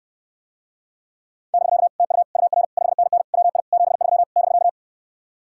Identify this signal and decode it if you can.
Morse code, heard as a keyed tone, about 69 wpm, 700 Hz. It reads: <BK> TU UR 5NN WI 73 <BK>